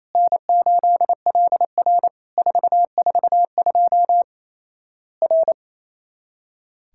Morse code, heard as a keyed tone, about 28 wpm, 700 Hz. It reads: N8LR 442 R
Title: Worked All Europe DX Contest - QTC Exchange - 28wpm